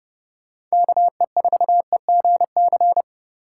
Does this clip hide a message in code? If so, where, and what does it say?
Yes: Morse code from 0.7 to 3.0 s, heard as a keyed tone, about 29 wpm, 700 Hz. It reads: KE4EGC